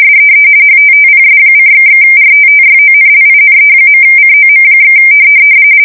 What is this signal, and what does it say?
RTTY, about 45.45 baud. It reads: SEE HTTPS://RTTYONLINE.COM/
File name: 60wpm / latest